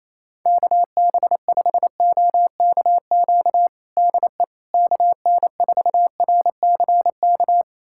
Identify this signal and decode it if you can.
Morse code, heard as a keyed tone, about 28 wpm, 700 Hz. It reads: KB5OKQ DE KN4RCK
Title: USA Callsign DE Callsign 28wpm